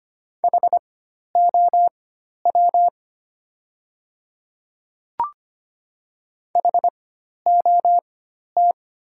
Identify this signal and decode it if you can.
Morse code, heard as a keyed tone, about 25 wpm, spaced as for 12 wpm, 700 Hz. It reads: HOW HOT
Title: 4x Character Spacing - Top 200 Words - 25wpm